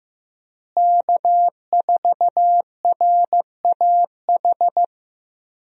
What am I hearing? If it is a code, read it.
Morse code, heard as a keyed tone, about 15 wpm, 700 Hz. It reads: K4RAH
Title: USA Call Signs - 5 Characters - Repeated 3x - 15wpm